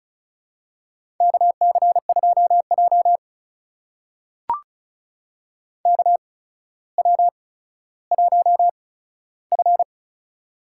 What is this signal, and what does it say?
Morse code, heard as a keyed tone, about 36 wpm, spaced as for 15 wpm, 700 Hz. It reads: KC2J KW1F
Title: USA Call Signs - 4 Characters - 8x Character Spacing - 35wpm